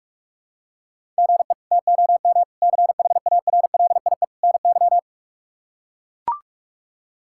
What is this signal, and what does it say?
Morse code, heard as a keyed tone, about 41 wpm, 700 Hz. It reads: GE TOM CHARLIE NY
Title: K1USN SST QSO Practice - 45wpm